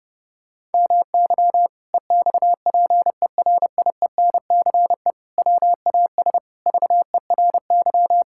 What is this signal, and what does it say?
Morse code, heard as a keyed tone, about 30 wpm, 700 Hz. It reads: MY EXPERIENCE WAS VERY
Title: Sentences from Top 1000 Words - Review Part 6 - 30wpm